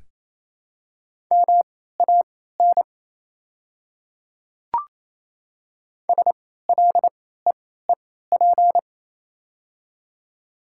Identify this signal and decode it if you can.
Morse code, heard as a keyed tone, about 28 wpm, spaced as for 16 wpm, 700 Hz. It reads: MAN SLEEP
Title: 3x Character Spacing - Top 500 Words - 28wpm